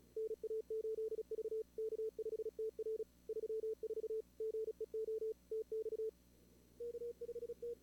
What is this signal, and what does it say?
Morse code, heard as a keyed tone, about 34 wpm, 440 Hz. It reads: NA8V K5TR 34 GEO TX K5T
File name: CWSprint / Feb2022 / k5tr-feb-na-sprint-cw-k5ot -2300